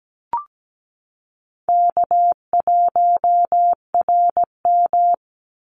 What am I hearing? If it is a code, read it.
Morse code, heard as a keyed tone, about 17 wpm, 700 Hz. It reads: K1RM
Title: Easy Call Signs - 17wpm